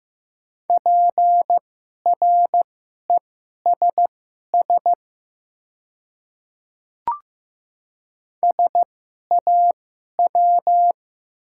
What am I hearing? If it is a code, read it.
Morse code, heard as a keyed tone, about 15 wpm, spaced as for 11 wpm, 700 Hz. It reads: PRESS SAW
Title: 2x Character Spacing - Top 500 Words - 15wpm